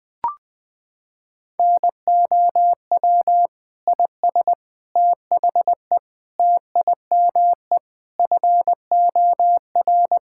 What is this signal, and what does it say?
Morse code, heard as a keyed tone, about 21 wpm, 700 Hz. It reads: NOW IS THE TIME FOR
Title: Pangram Challenge v2 - 20wpm